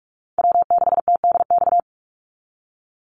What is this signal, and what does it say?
Morse code, heard as a keyed tone, about 44 wpm, 700 Hz. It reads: W6TDX
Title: NA SOTA Chaser Callsigns 45wpm